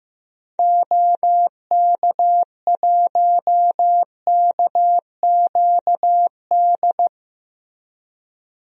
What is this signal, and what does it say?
Morse code, heard as a keyed tone, about 15 wpm, 700 Hz. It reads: OK1KQD
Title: DXCC-100 Call Signs - 6 Characters - 15wpm